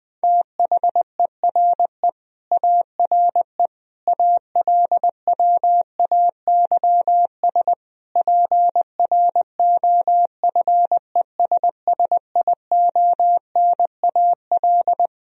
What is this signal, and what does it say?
Morse code, heard as a keyed tone, about 21 wpm, 700 Hz. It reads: THERE ARE ALWAYS PROFESSIONAL
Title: Sentences from Top 1000 Words - Review Part 4 - 20wpm